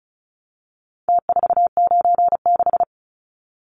Morse code, heard as a keyed tone, about 34 wpm, 700 Hz. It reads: T496
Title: Numbers - Contest Style - 4-Digit 35wpm